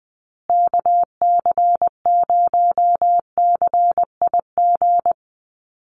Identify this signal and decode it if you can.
Morse code, heard as a keyed tone, about 20 wpm, 700 Hz. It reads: KC0CIG
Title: USA Call Signs Repeated 2x 20wpm